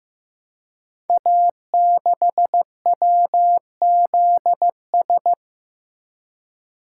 Morse code, heard as a keyed tone, about 15 wpm, 700 Hz. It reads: A6WZS
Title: Code Groups - 5 Letter:Numbers - 15wpm